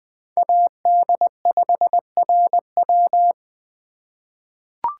Morse code, heard as a keyed tone, about 20 wpm, 700 Hz. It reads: AD5RW
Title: USA Call Signs - 5 Characters - 2x Character Spacing - 20wpm